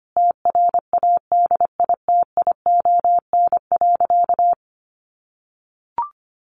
Morse code, heard as a keyed tone, about 25 wpm, 700 Hz. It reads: TRADITION.
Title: Sentences from Top 1500 Words 25wpm